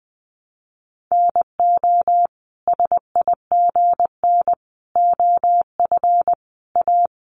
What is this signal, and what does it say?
Morse code, heard as a keyed tone, about 20 wpm, 700 Hz. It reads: NO SIGN OF A
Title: Sentences from Top 1000 Words 20wpm